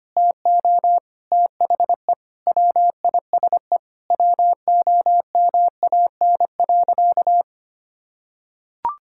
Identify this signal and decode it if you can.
Morse code, heard as a keyed tone, about 25 wpm, 700 Hz. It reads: TO THE WISE WOMAN.
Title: Sentences from Top 1600 Words 25wpm